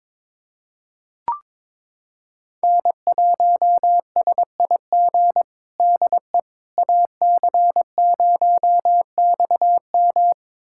Morse code, heard as a keyed tone, about 22 wpm, 700 Hz. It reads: N1SIG DE AC0XM
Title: USA Callsign DE Callsign - 5 Characters - 22wpm